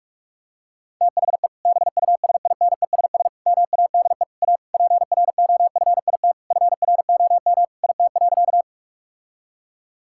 Morse code, heard as a keyed tone, about 46 wpm, 700 Hz. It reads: THE BUSINESS MADE A PROFIT FROM IT.
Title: Sentences from Top 1000 Words - Review Part 5 - 45wpm